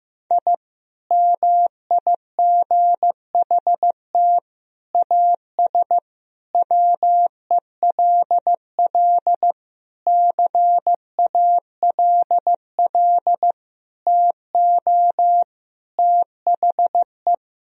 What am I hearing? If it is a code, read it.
Morse code, heard as a keyed tone, about 15 wpm, 700 Hz. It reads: I MIGHT AS WELL CALL TO THE
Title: Sentences from Top 1000 Words - Review Part 2 - 15wpm